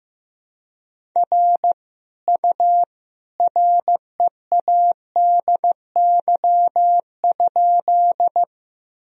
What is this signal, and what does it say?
Morse code, heard as a keyed tone, about 15 wpm, 700 Hz. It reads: R U READY?
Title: Sentences from Top 400 Words and Abbreviations 15wpm